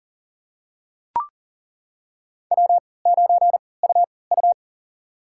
Morse code, heard as a keyed tone, about 40 wpm, spaced as for 23 wpm, 700 Hz. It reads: W9UU
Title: USA Call Signs - 4 Characters - 3x Character Spacing - 40wpm